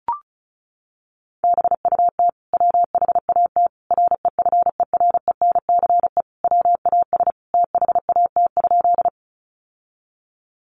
Morse code, heard as a keyed tone, about 34 wpm, 700 Hz. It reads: BUT WHAT REFERENCE WAS THAT?
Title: Sentences from Top 1200 Words 35wpm